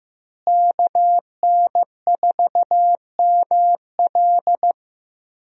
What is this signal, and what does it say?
Morse code, heard as a keyed tone, about 15 wpm, 700 Hz. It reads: KN4ML
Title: USA Call Signs - 5 Characters - Repeated 2x - 15wpm